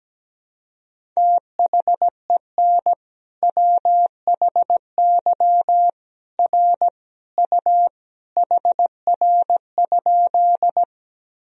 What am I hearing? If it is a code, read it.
Morse code, heard as a keyed tone, about 17 wpm, 700 Hz. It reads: THEN WHY R U HR?
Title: Sentences from Top 200 Words and Abbreviations 17wpm